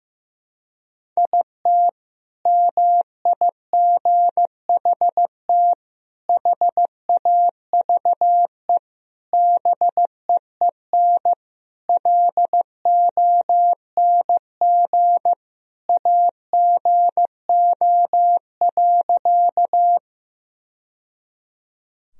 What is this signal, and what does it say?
Morse code, heard as a keyed tone, about 15 wpm, 700 Hz. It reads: IT MIGHT HAVE BEEN LONG AGO.
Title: Sentences from Top 1000 Words - Review Part 3 - 15wpm